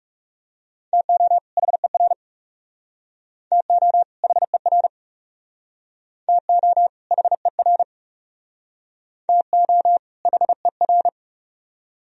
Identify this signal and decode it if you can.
Morse code, heard as a keyed tone, about 37 wpm, 700 Hz. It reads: TO HER TO HER TO HER TO HER
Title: Speed-Racing - Sets of 2 Words 30wpm